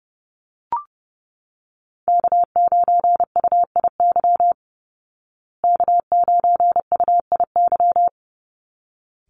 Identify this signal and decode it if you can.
Morse code, heard as a keyed tone, about 30 wpm, 700 Hz. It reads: K9UIY K9UIY
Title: Call Signs Repeated 3x - 30wpm